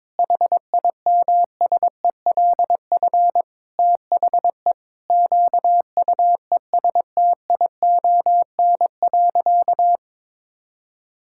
Morse code, heard as a keyed tone, about 22 wpm, 700 Hz. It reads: HIMSELF THE QUESTION.
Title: Sentences from Top 1600 Words 22wpm